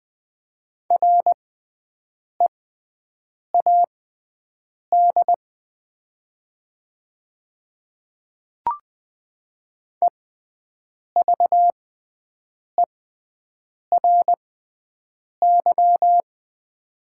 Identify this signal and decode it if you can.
Morse code, heard as a keyed tone, about 21 wpm, spaced as for 7 wpm, 700 Hz. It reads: READ EVERY
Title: 6x Character Spacing - Top 200 Words - 20wpm